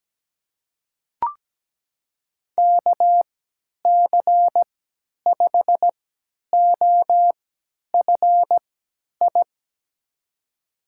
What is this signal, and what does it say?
Morse code, heard as a keyed tone, about 17 wpm, spaced as for 10 wpm, 700 Hz. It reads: KC5OFI
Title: USA Call Signs - 6 Characters - 3x Character Spacing - 17wpm